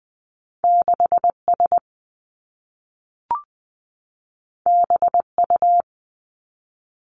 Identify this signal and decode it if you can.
Morse code, heard as a keyed tone, about 20 wpm, 700 Hz. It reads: 6 S B U
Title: Two Challenging Letter-Number Combinations 20wpm